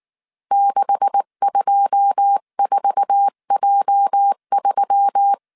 Morse code, heard as a keyed tone, about 19 wpm, 800 Hz. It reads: 624J3
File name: CW / DET / 1 9 1500C S B 18wpm 2551